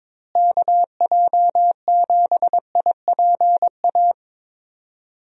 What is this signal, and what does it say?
Morse code, heard as a keyed tone, about 22 wpm, 700 Hz. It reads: KJ7IPA
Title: USA Call Signs - 6 Characters - Repeated 2x - 22wpm